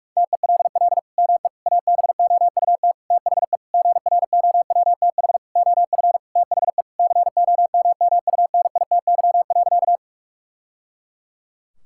Morse code, heard as a keyed tone, about 46 wpm, 700 Hz. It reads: TELL ME ABOUT THE GROWTH OF THE COMMUNITY.
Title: Sentences from Top 1000 Words - Review Part 5 - 45wpm